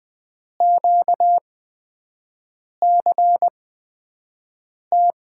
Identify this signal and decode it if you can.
Morse code, heard as a keyed tone, about 20 wpm, spaced as for 5 wpm, 700 Hz. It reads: QCT
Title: USA Call Signs - 6 Characters - 8x Character Spacing - 20wpm